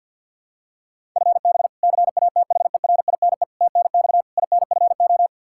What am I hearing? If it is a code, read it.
Morse code, heard as a keyed tone, about 45 wpm, 700 Hz. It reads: FB CATHERINE TNX INFO
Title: K1USN SST QSO Practice 50wpm